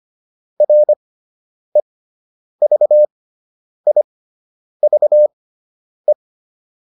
Morse code, heard as a keyed tone, about 25 wpm, spaced as for 9 wpm, 600 Hz. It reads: REVIVE